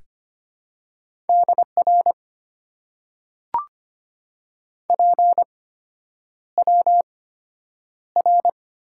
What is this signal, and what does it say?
Morse code, heard as a keyed tone, about 25 wpm, spaced as for 9 wpm, 700 Hz. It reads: DR PWR
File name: Top 90 QSO Elements - 8x Character Spacing - 25wpm